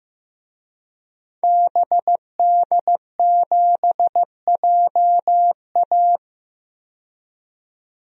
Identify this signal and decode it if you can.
Morse code, heard as a keyed tone, about 15 wpm, 700 Hz. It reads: BD7JA
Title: DXCC-100 Call Signs - 5 Characters - 15wpm